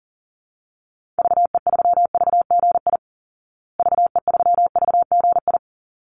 Morse code, heard as a keyed tone, about 40 wpm, 700 Hz. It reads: VE3VGI VE3VGI
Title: Call Signs Repeated 3x - 40wpm